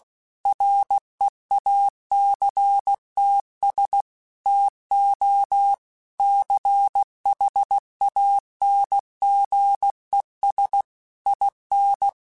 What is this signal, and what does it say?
Morse code, heard as a keyed tone, about 16 wpm, 800 Hz. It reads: REACTS TO CHANGES IN
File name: Technical Subjects / Technical Subjects-mp3s / Reactance @15